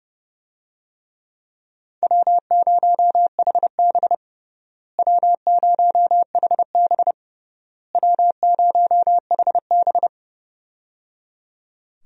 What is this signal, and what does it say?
Morse code, heard as a keyed tone, about 30 wpm, 700 Hz. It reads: W0HB W0HB W0HB
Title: USA Call Signs - 4 Characters - Repeated 3x - 30wpm